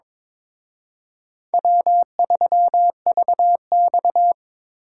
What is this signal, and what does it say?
Morse code, heard as a keyed tone, about 26 wpm, 700 Hz. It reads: W3VX
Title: Speed Racing - USA Call Signs 22wpm